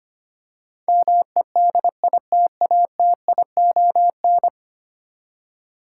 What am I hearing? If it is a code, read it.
Morse code, heard as a keyed tone, about 25 wpm, 700 Hz. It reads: MEDITATION
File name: Ten Letter Words - 25wpm